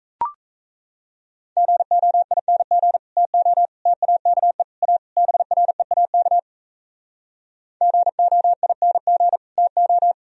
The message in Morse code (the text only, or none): GOING TO TAKE A BREAK GOING TO